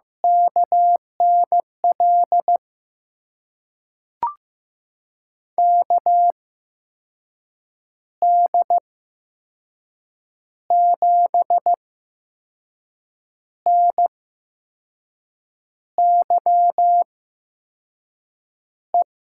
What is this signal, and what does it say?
Morse code, heard as a keyed tone, about 15 wpm, spaced as for 5 wpm, 700 Hz. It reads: KNL KD7NYE
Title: USA Call Signs - 6 Characters - 8x Character Spacing - 15wpm